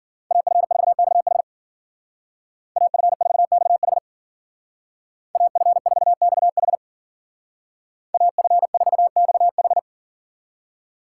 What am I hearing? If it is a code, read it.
Morse code, heard as a keyed tone, about 49 wpm, 700 Hz. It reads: AF4XH AF4XH AF4XH AF4XH
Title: Speed Racing - USA 2x2 Call Signs 40wpm